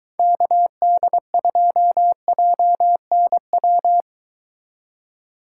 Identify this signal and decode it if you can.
Morse code, heard as a keyed tone, about 22 wpm, 700 Hz. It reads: KD2JNW